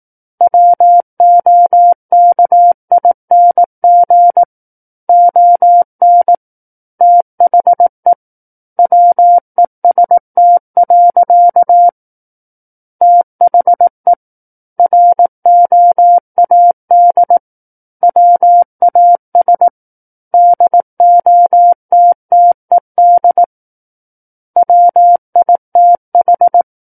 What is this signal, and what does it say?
Morse code, heard as a keyed tone, about 18 wpm, 700 Hz. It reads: WOKING ON THE WEST. THE ROAD WAS DOTTED WITH
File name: Book One / 09-Chapter 9 - 18 WPM